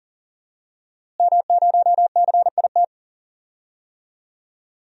Morse code, heard as a keyed tone, about 40 wpm, 700 Hz. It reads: M0CIT
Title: DXCC-100 Call Signs - 5 Characters - Repeated 3x - 40wpm